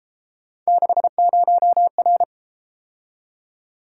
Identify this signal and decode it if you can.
Morse code, heard as a keyed tone, about 32 wpm, 700 Hz. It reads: 60R